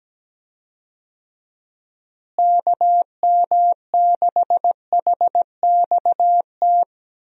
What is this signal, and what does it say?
Morse code, heard as a keyed tone, about 17 wpm, 700 Hz. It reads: KM6HXT
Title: USA Call Signs - 6 Characters - Repeated 2x - 17wpm